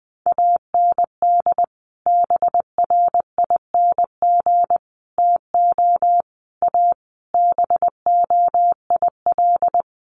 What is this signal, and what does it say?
Morse code, heard as a keyed tone, about 20 wpm, 700 Hz. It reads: AND BRING TO A BOIL
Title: Sets of 5 Words Encore 20wpm